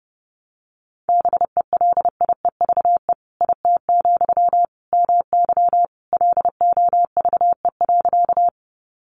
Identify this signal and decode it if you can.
Morse code, heard as a keyed tone, about 30 wpm, 700 Hz. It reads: BELIEVE IT, MY LOVE.
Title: Sentences from Top 1000 Words 30wpm